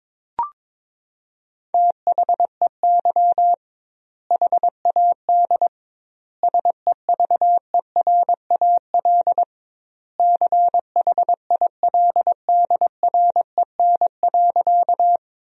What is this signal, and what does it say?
Morse code, heard as a keyed tone, about 22 wpm, 700 Hz. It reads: THEY HAD SEVERAL CHILDREN.
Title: 2x Word-Spacing - Sentences from Top 400 Words - 22wpm